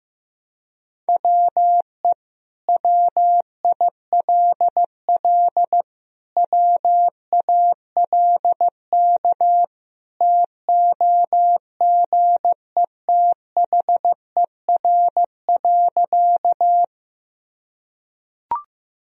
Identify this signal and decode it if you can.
Morse code, heard as a keyed tone, about 15 wpm, 700 Hz. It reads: WE WILL WALK TOGETHER.
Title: Sentences from Top 1000 Words - Review Part 1 - 15wpm